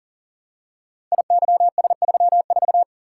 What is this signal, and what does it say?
Morse code, heard as a keyed tone, about 40 wpm, 700 Hz. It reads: IYS34
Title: Code Groups - 5 Letter:Numbers - 40wpm